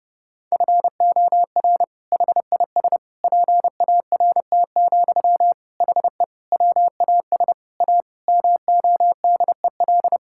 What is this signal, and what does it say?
Morse code, heard as a keyed tone, about 30 wpm, 700 Hz. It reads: FOR HIS PART, HE WAS A MODEL
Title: Sentences from Top 1000 Words - Review Part 5 - 30wpm